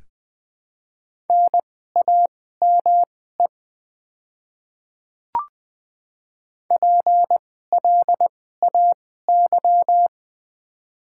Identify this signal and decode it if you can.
Morse code, heard as a keyed tone, about 20 wpm, spaced as for 15 wpm, 700 Hz. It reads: NAME PLAY
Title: 2x Character Spacing - Top 200 Words - 20wpm